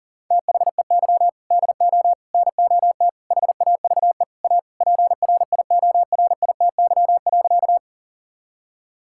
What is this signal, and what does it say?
Morse code, heard as a keyed tone, about 40 wpm, 700 Hz. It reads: THEY DO NOT HAVE A PRIORITY.
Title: Sentences from Top 1600 Words 40wpm